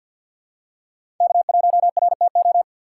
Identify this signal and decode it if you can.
Morse code, heard as a keyed tone, about 50 wpm, 700 Hz. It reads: K1RTO
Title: USA Call Signs - 5 Characters - 4x Character Spacing - 50wpm